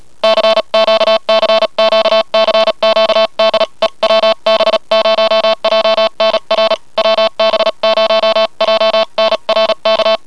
Morse code, heard as a keyed tone, about 34 wpm, 660 Hz. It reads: CQ CQ CQ DE WB0JNR WB0JNR K
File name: hearcw / cw cq by wb0jnr